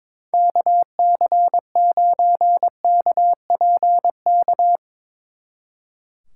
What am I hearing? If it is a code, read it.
Morse code, heard as a keyed tone, about 24 wpm, 700 Hz. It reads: KC9KPK
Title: Speed Racing - USA Call Signs 22wpm